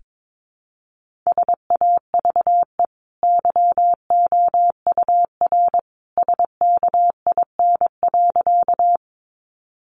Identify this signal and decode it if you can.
Morse code, heard as a keyed tone, about 22 wpm, 700 Hz. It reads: SAVE YOUR SKIN.
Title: Sentences from Top 1300 Words 22wpm